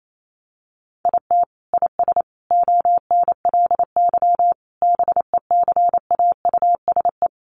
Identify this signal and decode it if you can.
Morse code, heard as a keyed tone, about 28 wpm, 700 Hz. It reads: IT IS ONLY BECAUSE
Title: Sentences from Top 1400 Words 28wpm